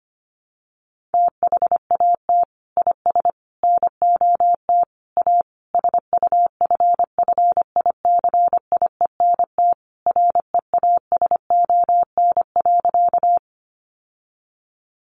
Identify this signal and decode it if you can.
Morse code, heard as a keyed tone, about 25 wpm, 700 Hz. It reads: THAT IS NOT A SUFFICIENT REASON.
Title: Sentences from Top 1400 Words 25wpm